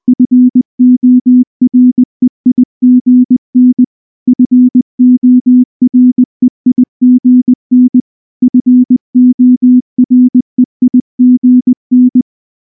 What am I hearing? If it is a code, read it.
Morse code, heard as a keyed tone, about 19 wpm, 250 Hz. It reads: FOREIGN FOREIGN FOREIGN
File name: morse / morse-training-20WPM-5